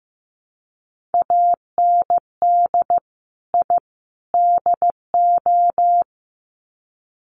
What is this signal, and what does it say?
Morse code, heard as a keyed tone, about 15 wpm, 700 Hz. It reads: AND I DO
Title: Sets of 3 Words 15wpm